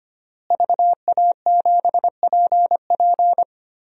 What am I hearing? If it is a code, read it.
Morse code, heard as a keyed tone, about 25 wpm, 700 Hz. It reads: VA7PP
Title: DXCC-100 Call Signs - 5 Characters - Repeated 3x - 25wpm